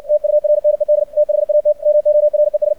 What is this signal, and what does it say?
Morse code, heard as a keyed tone, about 56 wpm, 600 Hz. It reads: TURNED THAT MORES
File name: archived / CRSnet